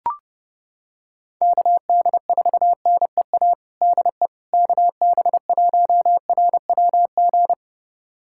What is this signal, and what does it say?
Morse code, heard as a keyed tone, about 30 wpm, 700 Hz. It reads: KD4NEA DE KB1RWG
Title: USA Callsign DE Callsign - 6 Characters - 30wpm